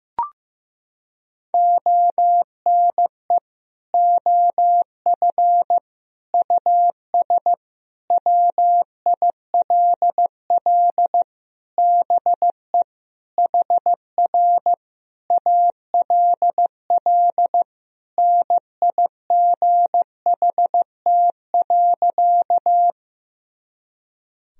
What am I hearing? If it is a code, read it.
Morse code, heard as a keyed tone, about 15 wpm, 700 Hz. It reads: ONE OF US WILL BE HR ALL NIGHT.
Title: Sentences from Top 300 Words and Abbreviations 15wpm